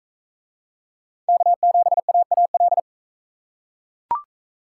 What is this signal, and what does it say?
Morse code, heard as a keyed tone, about 41 wpm, 700 Hz. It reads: K7AAL